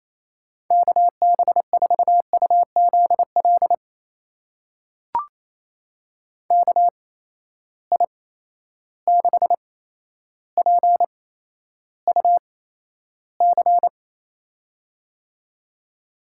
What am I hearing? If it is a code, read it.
Morse code, heard as a keyed tone, about 28 wpm, spaced as for 12 wpm, 700 Hz. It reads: KB4UZL KI6PUC
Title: USA Call Signs - 6 Characters - 8x Character Spacing - 28wpm